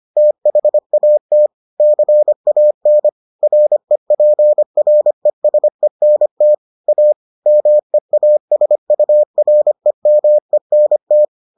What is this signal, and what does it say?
Morse code, heard as a keyed tone, about 25 wpm, 600 Hz. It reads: THAT CAN REPRESENT A MEASUREMENT